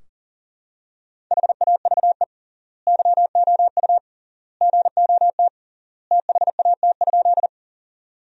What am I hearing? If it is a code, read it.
Morse code, heard as a keyed tone, about 40 wpm, 700 Hz. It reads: HAVE YOU GOT THAT?
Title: 3x Word-Spacing - Sentences from Top 300 Words - 40wpm